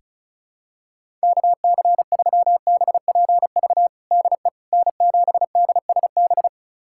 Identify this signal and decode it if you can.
Morse code, heard as a keyed tone, about 36 wpm, 700 Hz. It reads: KC3BPV DE N7DSB
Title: USA Callsign DE Callsign 35wpm